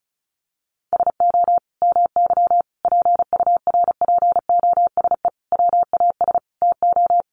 Morse code, heard as a keyed tone, about 34 wpm, 700 Hz. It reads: SO MY PURPOSE WAS TO
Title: Sentences from Top 1100 Words 35wpm